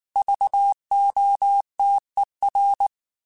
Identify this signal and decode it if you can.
Morse code, heard as a keyed tone, about 19 wpm, 800 Hz. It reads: VOTER